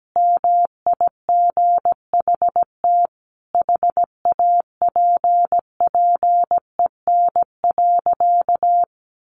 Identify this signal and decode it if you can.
Morse code, heard as a keyed tone, about 17 wpm, 700 Hz. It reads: MIGHT HAPPEN.
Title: Sentences from Top 400 Words 17wpm